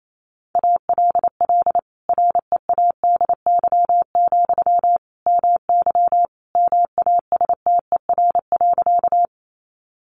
Morse code, heard as a keyed tone, about 28 wpm, 700 Hz. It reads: ALL READY, MY MASTER.
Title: Sentences from Top 1300 Words 28wpm